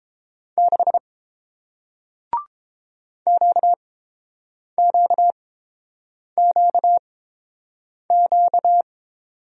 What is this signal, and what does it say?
Morse code, heard as a keyed tone, about 30 wpm, spaced as for 7 wpm, 700 Hz. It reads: 6 QQQQ